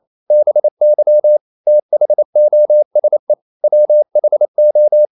Morse code, heard as a keyed tone, about 28 wpm, 600 Hz. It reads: BY THOSE WHO
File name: CW / Practice / EnglishQuotations0128